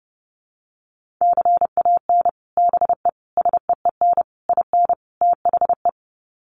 Morse code, heard as a keyed tone, about 30 wpm, 700 Hz. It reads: CAN BE SEEN IN THE
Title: Sets of 5 Words Encore2 30wpm